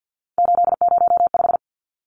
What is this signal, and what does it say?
Morse code, heard as a keyed tone, about 50 wpm, 700 Hz. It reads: 805
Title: Numbers - 3-Digit 50wpm